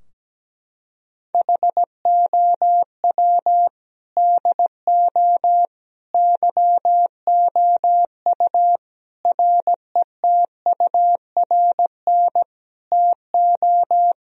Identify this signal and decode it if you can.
Morse code, heard as a keyed tone, about 17 wpm, 700 Hz. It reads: HOW DO YOU RETURN TO
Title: Sentences from Top 1000 Words - Review Part 3 - 17wpm